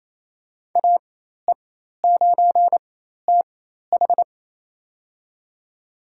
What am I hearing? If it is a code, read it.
Morse code, heard as a keyed tone, about 28 wpm, spaced as for 13 wpm, 700 Hz. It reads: AE9TH
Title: USA Call Signs - 4x Character Spacing - 28wpm